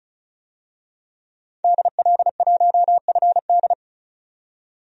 Morse code, heard as a keyed tone, about 36 wpm, 700 Hz. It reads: DL1FD
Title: DXCC-100 Call Signs - 5 Characters - 35wpm